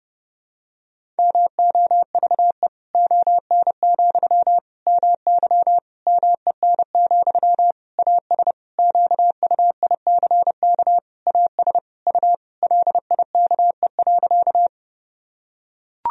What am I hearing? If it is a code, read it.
Morse code, heard as a keyed tone, about 32 wpm, 700 Hz. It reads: MOVE ON, MY MEN, AS QUICK AS U LIKE.
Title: Sentences from Top 500 Words and Abbreviations 30wpm